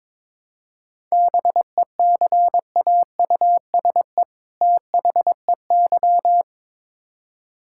Morse code, heard as a keyed tone, about 22 wpm, 700 Hz. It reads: BECAUSE THEY
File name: Build A Phrase - 2-Word Phrases - 22wpm